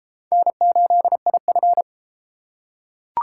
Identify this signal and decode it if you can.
Morse code, heard as a keyed tone, about 30 wpm, 700 Hz. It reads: N8IF